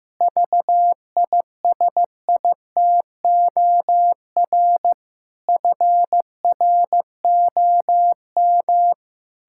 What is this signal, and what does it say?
Morse code, heard as a keyed tone, about 15 wpm, 700 Hz. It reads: VISITOR FROM
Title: Pangram Challenge v2 - 15wpm